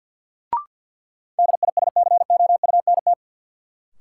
Morse code, heard as a keyed tone, about 48 wpm, 700 Hz. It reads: DISCOUNT
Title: Sentences from Top 1200 Words - New Word Drill Alt - 50wpm